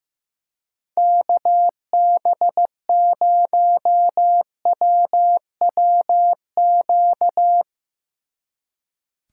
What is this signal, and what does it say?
Morse code, heard as a keyed tone, about 16 wpm, 700 Hz. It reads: KB0WWQ
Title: Speed Racing - USA Call Signs 15wpm